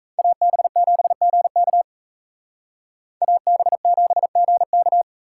Morse code, heard as a keyed tone, about 36 wpm, 700 Hz. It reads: AB7GK AB7GK